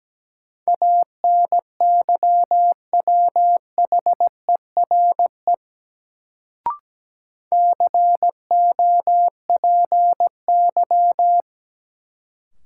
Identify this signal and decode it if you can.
Morse code, heard as a keyed tone, about 17 wpm, 700 Hz. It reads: ANYWHERE COPY
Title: Sentences from Top 1400 Words - New Word Drill - 17wpm